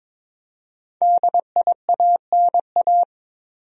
Morse code, heard as a keyed tone, about 22 wpm, 700 Hz. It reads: DIANA
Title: USA Common Names - Max 5 Letters - 22wpm